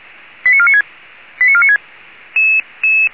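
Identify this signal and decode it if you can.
Five-tone selective call (ZVEI), heard as quick runs of five tones, tones of about 70 ms. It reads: 89496 89496
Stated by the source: Five-tone ZVEI selective call received by SDR, from audio.f85360.feuerwehr-orlamuende.org, a fire brigade (Feuerwehr Orlamünde).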